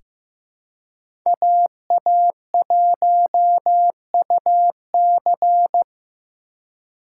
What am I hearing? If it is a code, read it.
Morse code, heard as a keyed tone, about 15 wpm, 700 Hz. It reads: AA1UC
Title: K1USN SST QSO Practice - 15wpm